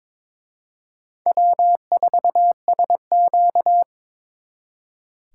Morse code, heard as a keyed tone, about 26 wpm, 700 Hz. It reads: W4SQ
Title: Speed Racing - USA 1x2 Call Signs 22wpm